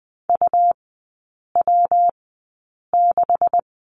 Morse code, heard as a keyed tone, about 20 wpm, spaced as for 8 wpm, 700 Hz. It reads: UW6
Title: 159 Lesson 26e - Letter g - Three Characters 20wpm